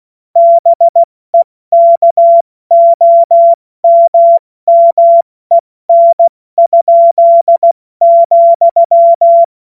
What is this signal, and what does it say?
Morse code, heard as a keyed tone, about 16 wpm, 680 Hz. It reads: BEKOMMEN?,
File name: Morsekurs / Witze / Witz 13. Juni 2023 70 F80